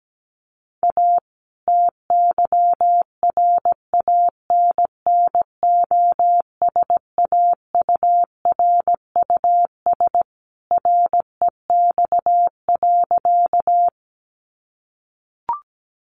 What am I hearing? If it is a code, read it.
Morse code, heard as a keyed tone, about 17 wpm, 700 Hz. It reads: A TYRANNOSAURUS REX.
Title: A Sound Of Thunder 17wpm